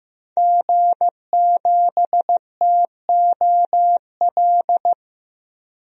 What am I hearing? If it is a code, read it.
Morse code, heard as a keyed tone, about 15 wpm, 700 Hz. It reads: G7TOL
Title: DXCC-100 Call Signs - 5 Characters - 15wpm